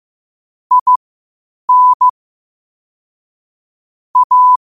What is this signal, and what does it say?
Morse code, heard as a keyed tone, about 15 wpm, spaced as for 8 wpm, 1000 Hz. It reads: IN A
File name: qotdfiles / 8wpm